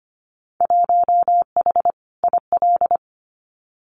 Morse code, heard as a keyed tone, about 25 wpm, 700 Hz. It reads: 1H IL
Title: Winter Field Day Exchanges 25wpm